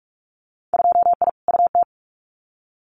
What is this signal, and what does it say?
Morse code, heard as a keyed tone, about 44 wpm, 700 Hz. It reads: 2I UT
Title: Winter Field Day Exchanges 45wpm